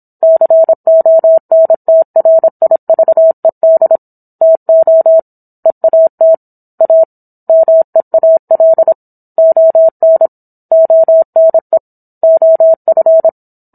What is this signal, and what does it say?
Morse code, heard as a keyed tone, about 27 wpm, 650 Hz. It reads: CONTRIVED TO EAT A MEAL ON ONE OF